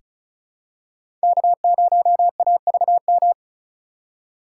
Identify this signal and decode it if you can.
Morse code, heard as a keyed tone, about 36 wpm, 700 Hz. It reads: K0AVM
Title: USA Call Signs - 5 Characters - 3x Character Spacing - 35wpm